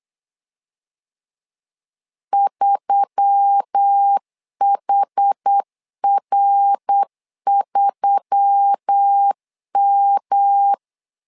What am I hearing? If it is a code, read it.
Morse code, heard as a keyed tone, about 8 wpm, 800 Hz. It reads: 3HR3M
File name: LEZIONI / 4 Q1S3 8wpm 3037